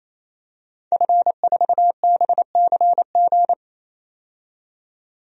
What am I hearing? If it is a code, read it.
Morse code, heard as a keyed tone, about 28 wpm, 700 Hz. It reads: F4BCG
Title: DXCC-100 Call Signs - 5 Characters - 28wpm